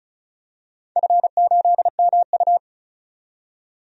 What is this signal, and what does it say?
Morse code, heard as a keyed tone, about 36 wpm, 700 Hz. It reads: F8MU
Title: DXCC-100 Call Signs - 4 Characters - 35wpm